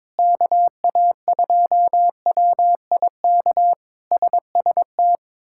Morse code, heard as a keyed tone, about 22 wpm, 700 Hz. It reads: KA2WIK SST
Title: K1USN SST QSO Practice - 22wpm